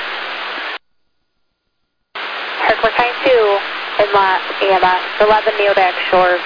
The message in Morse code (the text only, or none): none